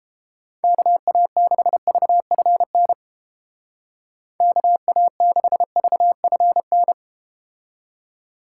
Morse code, heard as a keyed tone, about 31 wpm, 700 Hz. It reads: KA6VFN KA6VFN